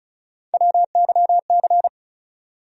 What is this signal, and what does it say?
Morse code, heard as a keyed tone, about 34 wpm, 700 Hz. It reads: WYC
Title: USA Call Sign Postfixes - 3 Letters 35wpm